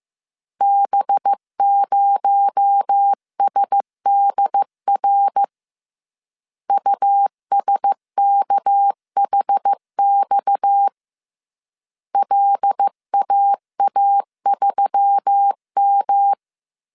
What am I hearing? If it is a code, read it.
Morse code, heard as a keyed tone, about 15 wpm, 800 Hz. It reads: B0SDR USKHX LAA3M